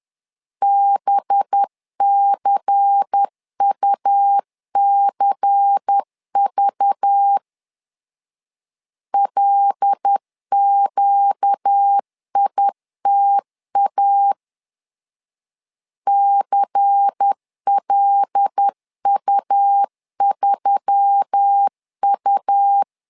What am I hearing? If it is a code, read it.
Morse code, heard as a keyed tone, about 10 wpm, 800 Hz. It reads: BCUCV LQITA CLU3U
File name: LEZIONI / 5 LBVC / 5 LBVC 10wpm 2349